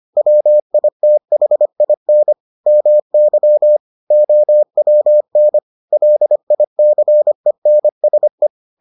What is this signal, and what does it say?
Morse code, heard as a keyed tone, about 25 wpm, 600 Hz. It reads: WITHIN MY OWN LICENSE